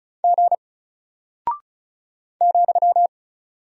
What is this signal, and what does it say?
Morse code, heard as a keyed tone, about 36 wpm, 700 Hz. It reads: G ,
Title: Single Character (Letter:Number:Punctuation:Prosign) Alt 35wpm